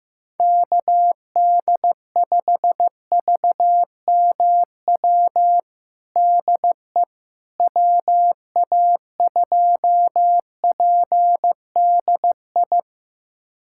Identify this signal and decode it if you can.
Morse code, heard as a keyed tone, about 15 wpm, 700 Hz. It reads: KD5VMW DE WA2PDI
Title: USA Callsign DE Callsign - 6 Characters - 15wpm